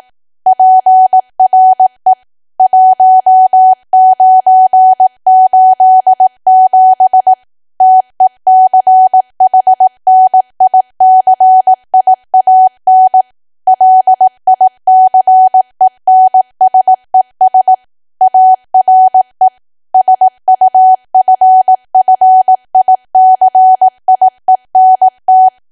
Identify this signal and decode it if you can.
Morse code, heard as a keyed tone, about 18 wpm, 750 Hz. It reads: PRE 1987 TECHNICIAN LICENSES ARE SUFFICIENT
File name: cwa / W1AW-Level2 / 18WPM